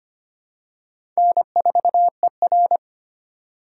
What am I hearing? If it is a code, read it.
Morse code, heard as a keyed tone, about 25 wpm, 700 Hz. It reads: N4ER
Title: USA Call Signs - 4 Characters - 3x Character Spacing - 25wpm